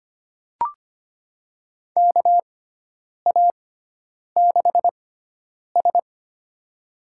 Morse code, heard as a keyed tone, about 25 wpm, spaced as for 9 wpm, 700 Hz. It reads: KA6S
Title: USA Call Signs - 4 Characters - 6x Character Spacing - 25wpm